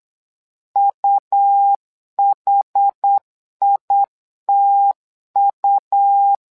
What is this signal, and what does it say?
Morse code, heard as a keyed tone, about 8 wpm, 800 Hz. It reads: UHITU
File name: LEZIONI / 2 UHDN / 2 UH 8wpm 2129